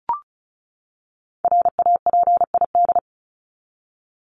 Morse code, heard as a keyed tone, about 34 wpm, 700 Hz. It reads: RAPID
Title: LY Suffix - Top 100 Words with Base Words 35wpm